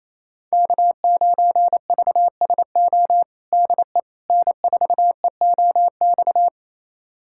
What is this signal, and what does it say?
Morse code, heard as a keyed tone, about 28 wpm, 700 Hz. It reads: K9VSO DE N4EOX
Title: USA Callsign DE Callsign - 5 Characters - 28wpm